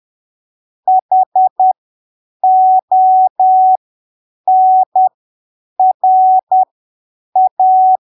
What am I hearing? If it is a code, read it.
Morse code, heard as a keyed tone, about 10 wpm, spaced as for 7 wpm, 750 Hz. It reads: HONRA
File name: arquivos / texto / texto 28 10ppm